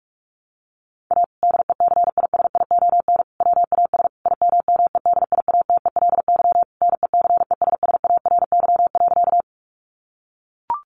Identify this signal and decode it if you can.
Morse code, heard as a keyed tone, about 44 wpm, 700 Hz. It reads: A DECISION WAS IMMEDIATELY NECESSARY.
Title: Sentences from Top 1100 Words 45wpm